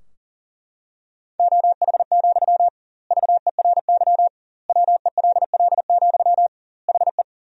Morse code, heard as a keyed tone, about 40 wpm, 700 Hz. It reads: OH, VERY WELL, HE
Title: 2x Word-Spacing - Sentences from Top 500 Words - 40wpm